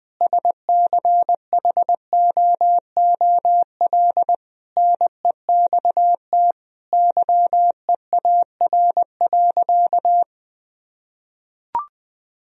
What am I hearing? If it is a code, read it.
Morse code, heard as a keyed tone, about 20 wpm, 700 Hz. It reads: SCHOOL NEXT YEAR.
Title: Sentences from Top 1000 Words - Review Part 3 - 20wpm